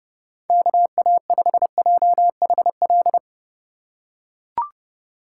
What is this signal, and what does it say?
Morse code, heard as a keyed tone, about 29 wpm, 700 Hz. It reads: KA5JHL